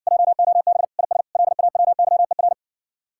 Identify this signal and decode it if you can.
Morse code, heard as a keyed tone, about 60 wpm, 700 Hz. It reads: JOB IS LAWYER